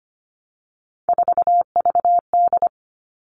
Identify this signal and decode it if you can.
Morse code, heard as a keyed tone, about 25 wpm, 700 Hz. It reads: 4VD
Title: All Cut Numbers - 3-Digit 25wpm